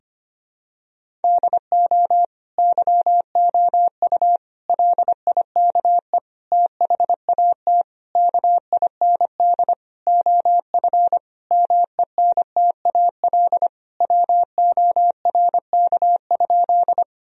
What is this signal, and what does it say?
Morse code, heard as a keyed tone, about 25 wpm, 700 Hz. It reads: DO YOU LIKE THAT KIND OF MENTAL WORK?
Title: Sentences from Top 1000 Words - Review Part 4 - 25wpm